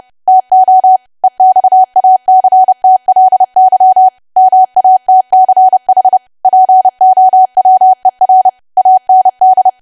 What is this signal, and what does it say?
Morse code, heard as a keyed tone, about 30 wpm, 750 Hz. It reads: TO EXACTLY MATCH POWER AND
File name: cwa / W1AW-Level3 / 30WPM